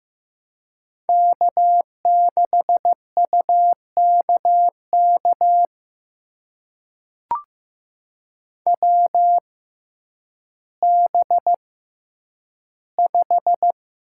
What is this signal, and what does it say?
Morse code, heard as a keyed tone, about 15 wpm, spaced as for 9 wpm, 700 Hz. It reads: K6UKK WB5
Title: USA Call Signs - 6x Character Spacing - 15wpm